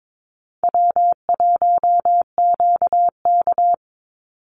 Morse code, heard as a keyed tone, about 22 wpm, 700 Hz. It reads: W1QK
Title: Easy Call Signs - 22wpm